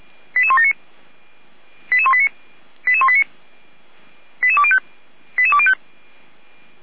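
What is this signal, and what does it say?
Five-tone selective call (ZVEI), heard as quick runs of five tones, tones of about 70 ms. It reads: 88179 88189 88189 88205 88205
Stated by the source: ZVEI five-tone call received by SDR, fire brigade Orlamünde (feuerwehr-orlamuende).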